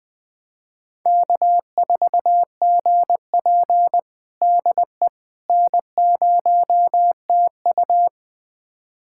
Morse code, heard as a keyed tone, about 20 wpm, 700 Hz. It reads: K4GP DE N0TU
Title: USA Callsign DE Callsign - 4 Characters - 20wpm